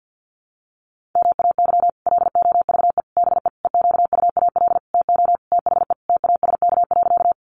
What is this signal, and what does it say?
Morse code, heard as a keyed tone, about 50 wpm, 700 Hz. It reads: MAY LOVE BE EQUAL TO THE TASK.
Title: Sentences from Top 1400 Words 50wpm